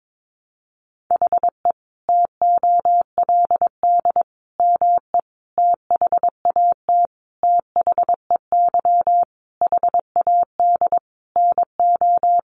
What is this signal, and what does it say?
Morse code, heard as a keyed tone, about 22 wpm, 700 Hz. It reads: HE TOLD ME THAT THEY HAD NO
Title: Sentences from Top 1300 Words 22wpm